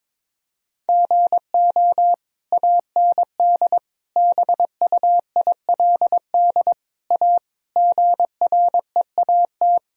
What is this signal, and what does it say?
Morse code, heard as a keyed tone, about 22 wpm, 700 Hz. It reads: GO AND BUILD A GREAT
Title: Sentences from Top 1000 Words - Review Part 1 - 22wpm